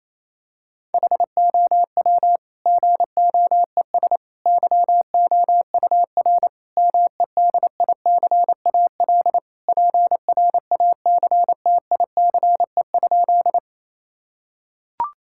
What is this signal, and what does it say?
Morse code, heard as a keyed tone, about 28 wpm, 700 Hz. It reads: HOW GOES YOUR MEDICAL PRACTICE?
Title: Sentences from Top 1000 Words - Review Part 5 - 28wpm